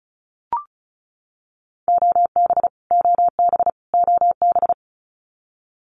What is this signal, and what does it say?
Morse code, heard as a keyed tone, about 34 wpm, 700 Hz. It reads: OB OB OB
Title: Abbreviations Repeated 3x - 35wpm